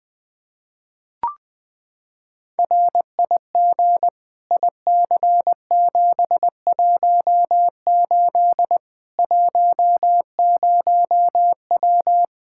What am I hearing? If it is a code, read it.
Morse code, heard as a keyed tone, about 20 wpm, 700 Hz. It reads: RIG IC718 10W
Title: Practice QSO - Exchange 2 - Info - 20wpm